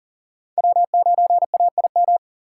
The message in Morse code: W9AIM

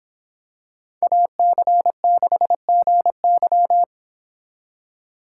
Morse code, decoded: AC6GY

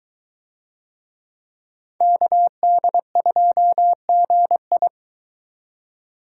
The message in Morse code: KD2GI

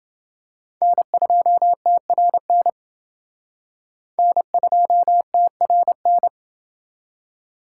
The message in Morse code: N2TRN N2TRN

